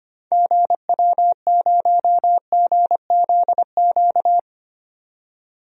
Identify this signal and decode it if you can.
Morse code, heard as a keyed tone, about 25 wpm, 700 Hz. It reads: GW0GZQ
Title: DXCC-100 Call Signs - 6 Characters - 25wpm